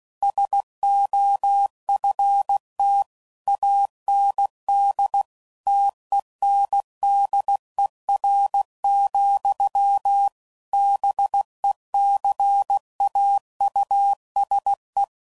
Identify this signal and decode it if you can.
Morse code, heard as a keyed tone, about 16 wpm, 800 Hz. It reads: SOFT AND TENDER, BECAUSE